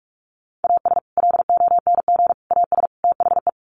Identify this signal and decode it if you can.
Morse code, heard as a keyed tone, about 44 wpm, 700 Hz. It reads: AS LONG AS THE
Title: Sets of 4 Words 45wpm